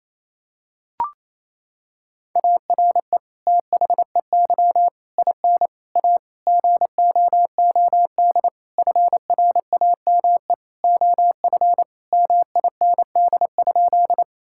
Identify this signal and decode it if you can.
Morse code, heard as a keyed tone, about 28 wpm, 700 Hz. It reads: ARE THEY IN A GOOD FRAME OF MIND?
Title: Sentences from Top 1000 Words - Review Part 6 - 28wpm